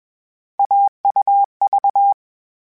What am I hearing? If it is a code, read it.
Morse code, heard as a keyed tone, about 21 wpm, 800 Hz. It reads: AUV
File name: Difficult Combos / Difficult Combos-mp3s / AUV ShortRun @20